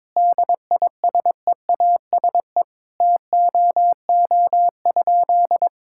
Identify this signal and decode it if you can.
Morse code, heard as a keyed tone, about 22 wpm, 700 Hz. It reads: DISEASE TOO?
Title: Sentences from Top 1000 Words - Review Part 6 - 22wpm